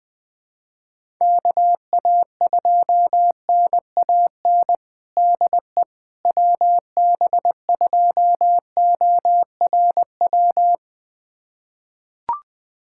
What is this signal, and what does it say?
Morse code, heard as a keyed tone, about 20 wpm, 700 Hz. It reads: KA2NAN DE WB2ORW
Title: USA Callsign DE Callsign 20wpm